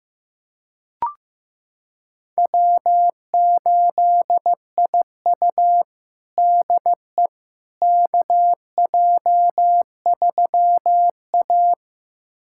Morse code, heard as a keyed tone, about 15 wpm, 700 Hz. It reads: W8IU DE KJ3A
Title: USA Callsign DE Callsign - 4 Characters - 15wpm